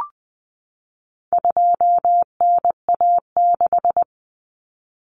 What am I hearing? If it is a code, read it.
Morse code, heard as a keyed tone, about 20 wpm, 700 Hz. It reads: 2NA6
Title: Common Cut Numbers - 4-Digit 20wpm